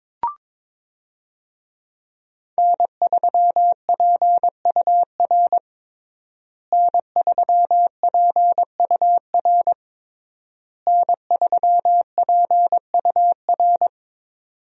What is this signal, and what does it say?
Morse code, heard as a keyed tone, about 21 wpm, 700 Hz. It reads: N3PUR N3PUR N3PUR
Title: USA Call Signs - 5 Characters - Repeated 3x - 22wpm